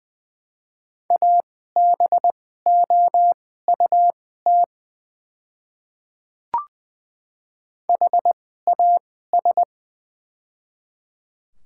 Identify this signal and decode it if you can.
Morse code, heard as a keyed tone, about 20 wpm, spaced as for 14 wpm, 700 Hz. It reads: ABOUT HAS
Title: 2x Character Spacing - Top 100 Words - 20wpm